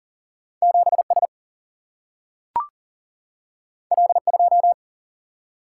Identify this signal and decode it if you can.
Morse code, heard as a keyed tone, about 40 wpm, 700 Hz. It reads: 7S L2
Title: Single-Letter Single-Number 40wpm